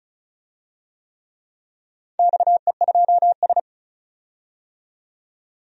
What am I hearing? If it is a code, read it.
Morse code, heard as a keyed tone, about 36 wpm, 700 Hz. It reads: XE2S
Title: DXCC-100 Call Signs - 4 Characters - 35wpm